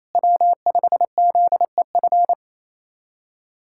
Morse code, heard as a keyed tone, about 28 wpm, 700 Hz. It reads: W5ZEF
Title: USA Callsign DE Callsign - 5 Characters - 28wpm